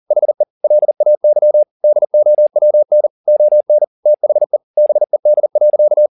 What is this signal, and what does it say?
Morse code, heard as a keyed tone, about 40 wpm, 600 Hz. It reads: HE LAY DOWN ON THE BED.
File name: the old man and the sea 2 40wpm